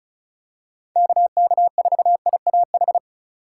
Morse code, heard as a keyed tone, about 36 wpm, 700 Hz. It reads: KK4IAH
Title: USA Call Signs - 6 Characters - 4x Character Spacing - 35wpm